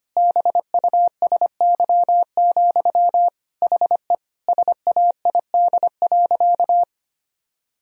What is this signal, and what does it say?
Morse code, heard as a keyed tone, about 25 wpm, 700 Hz. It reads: BUSY, HE SAID.
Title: Sentences from Top 1000 Words - Review Part 3 - 25wpm